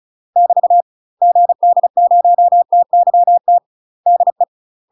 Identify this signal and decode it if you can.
Morse code, heard as a keyed tone, about 34 wpm, 700 Hz. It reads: <BT> GD0TYT DE
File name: UK MASTER QSOs TRAINING / 35WPM / UK MASTER QSOs 0043